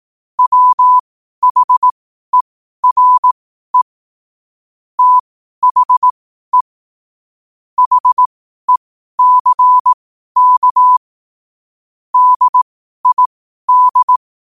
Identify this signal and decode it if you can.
Morse code, heard as a keyed tone, about 18 wpm, spaced as for 12 wpm, 1000 Hz. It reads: WHERE THE HECK DID
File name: qotdfiles / 12wpm